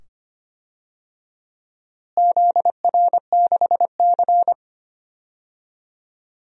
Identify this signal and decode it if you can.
Morse code, heard as a keyed tone, about 25 wpm, 700 Hz. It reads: ZR6C
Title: DXCC-100 Call Signs - 4 Characters - Repeated 2x - 25wpm